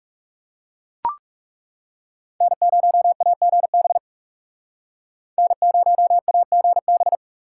Morse code, heard as a keyed tone, about 36 wpm, 700 Hz. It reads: N0AGB N0AGB
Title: Speed Racing - USA 1x3 Call Signs 30wpm